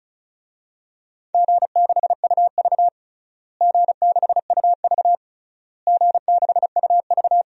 Morse code, heard as a keyed tone, about 36 wpm, 700 Hz. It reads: G6UV G6UV G6UV
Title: DXCC-100 Call Signs - 4 Characters - Repeated 3x - 35wpm